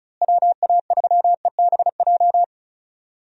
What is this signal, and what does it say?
Morse code, heard as a keyed tone, about 34 wpm, 700 Hz. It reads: WA3EBJ
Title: USA Call Signs - 6 Characters - Repeated 3x - 35wpm